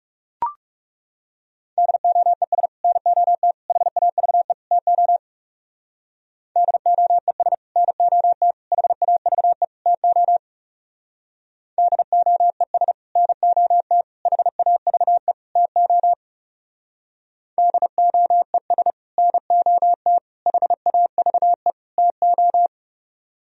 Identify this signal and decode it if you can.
Morse code, heard as a keyed tone, about 37 wpm, 700 Hz. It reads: DOES NOT HAVE TO DOES NOT HAVE TO DOES NOT HAVE TO DOES NOT HAVE TO
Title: Speed-Racing - Sets of 4 Words 30wpm